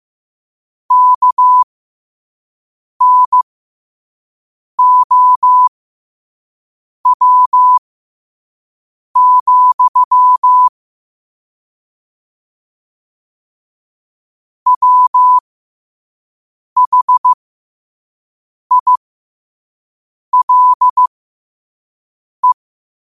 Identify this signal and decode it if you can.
Morse code, heard as a keyed tone, about 15 wpm, spaced as for 5 wpm, 1000 Hz. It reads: KNOW, WHILE
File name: qotdfiles / 5wpm